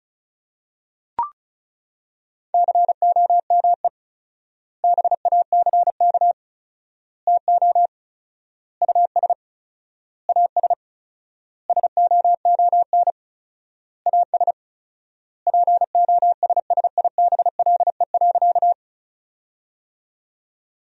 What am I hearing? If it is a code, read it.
Morse code, heard as a keyed tone, about 34 wpm, 700 Hz. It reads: COME BACK TO US AS SOON AS POSSIBLE.
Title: 4x Word-Spacing - Sentences from Top 500 Words - 35wpm